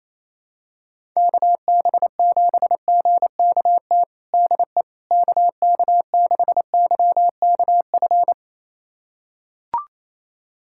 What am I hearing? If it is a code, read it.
Morse code, heard as a keyed tone, about 28 wpm, 700 Hz. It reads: KB7GKT DE KK6YKF
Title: USA Callsign DE Callsign 28wpm